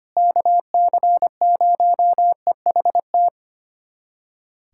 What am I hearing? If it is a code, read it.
Morse code, heard as a keyed tone, about 25 wpm, 700 Hz. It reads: KC0EHT
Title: USA Callsign DE Callsign 25wpm